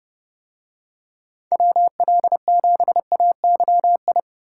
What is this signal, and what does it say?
Morse code, heard as a keyed tone, about 28 wpm, 700 Hz. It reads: WL7AYI